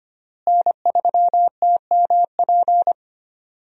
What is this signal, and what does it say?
Morse code, heard as a keyed tone, about 25 wpm, 700 Hz. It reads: N3TMP
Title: USA Call Signs - 5 Characters - Repeated 2x - 25wpm